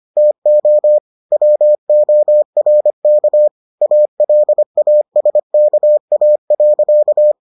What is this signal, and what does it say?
Morse code, heard as a keyed tone, about 25 wpm, 600 Hz. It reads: TO WORK ALASKA.